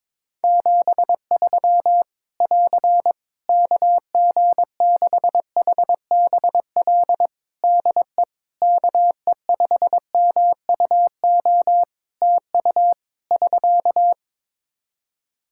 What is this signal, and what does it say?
Morse code, heard as a keyed tone, about 22 wpm, 700 Hz. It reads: 73 <AR> KG6HBL DE KE5MUO TU <SK>
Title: Practice QSO - Exchange 4 - Ending - 22wpm